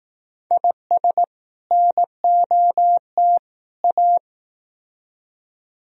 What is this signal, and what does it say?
Morse code, heard as a keyed tone, about 18 wpm, 700 Hz. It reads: IS NOT A
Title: Speed-Racing - Sets of 3 Words 15wpm